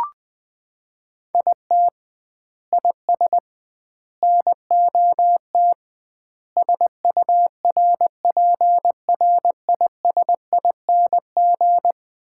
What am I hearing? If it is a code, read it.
Morse code, heard as a keyed tone, about 20 wpm, 700 Hz. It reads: IT IS NOT SURPRISING
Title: 2x Word-Spacing - Sets of 5 Words - 20wpm